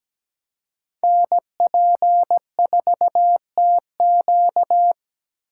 Morse code, heard as a keyed tone, about 20 wpm, 700 Hz. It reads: NP4TQ
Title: Speed Racing - USA 2x2 Call Signs 17wpm